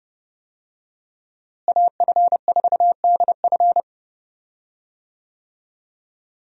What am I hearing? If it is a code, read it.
Morse code, heard as a keyed tone, about 30 wpm, 700 Hz. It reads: AF4DF
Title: USA Call Signs - 5 Characters - Repeated 2x - 30wpm